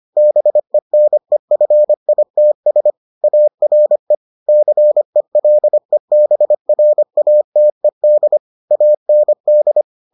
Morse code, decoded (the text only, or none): BENEFITS ARE CELEBRATED AND